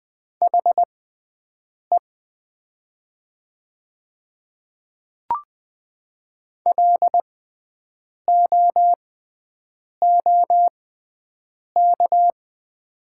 Morse code, heard as a keyed tone, about 20 wpm, spaced as for 7 wpm, 700 Hz. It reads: HE LOOK
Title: 6x Character Spacing - Top 500 Words - 20wpm